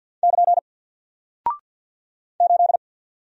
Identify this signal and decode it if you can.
Morse code, heard as a keyed tone, about 48 wpm, 700 Hz. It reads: <KN> 8
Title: Single Character (Letter:Number:Punctuation:Prosign) Alt 50wpm